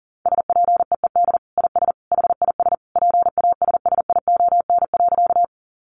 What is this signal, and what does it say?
Morse code, heard as a keyed tone, about 40 wpm, 700 Hz. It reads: SPEED IS HIS PASSION.
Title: Sentences from Top 1300 Words 40wpm